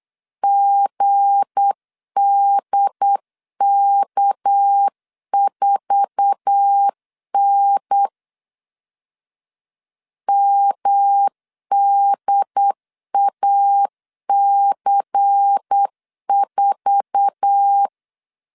Morse code, heard as a keyed tone, about 8 wpm, 800 Hz. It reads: GDK4N MDAC4
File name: LEZIONI / 8 YJ45 / 8 YJ4 8wpm 3246